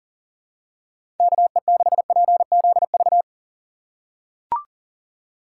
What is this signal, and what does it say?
Morse code, heard as a keyed tone, about 40 wpm, 700 Hz. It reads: KE6PZV
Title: USA Call Signs - 3x Character Spacing - 40wpm